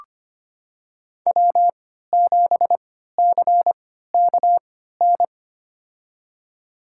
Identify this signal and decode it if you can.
Morse code, heard as a keyed tone, about 25 wpm, spaced as for 14 wpm, 700 Hz. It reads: W7CKN
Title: USA Call Signs - 3x Character Spacing - 25wpm